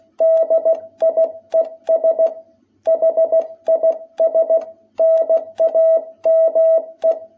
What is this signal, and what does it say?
Morse code, heard as a keyed tone, about 16 wpm, 660 Hz. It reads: DIES HIS NAME
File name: jh8cbh / 2021cw